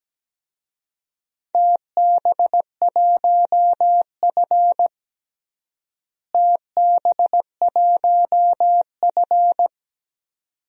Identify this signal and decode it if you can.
Morse code, heard as a keyed tone, about 17 wpm, 700 Hz. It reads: TB1F TB1F
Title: DXCC-100 Call Signs - 4 Characters - Repeated 2x - 17wpm